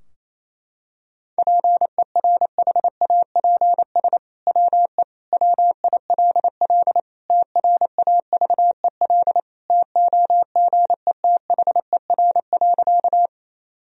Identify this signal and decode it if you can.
Morse code, heard as a keyed tone, about 27 wpm, 700 Hz. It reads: PERHAPS WE WILL TRAVEL TOGETHER.
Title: Sentences from Top 1000 Words - Review Part 2 - 28wpm